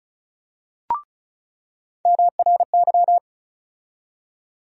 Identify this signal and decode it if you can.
Morse code, heard as a keyed tone, about 36 wpm, 700 Hz. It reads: MRY
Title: USA Call Sign Postfixes - 3 Letters 35wpm